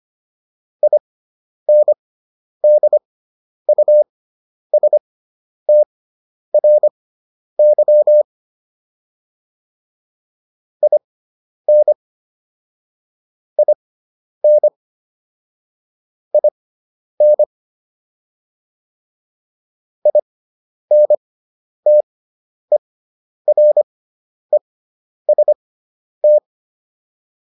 Morse code, heard as a keyed tone, about 25 wpm, spaced as for 10 wpm, 600 Hz. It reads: INDUSTRY IN IN IN INTEREST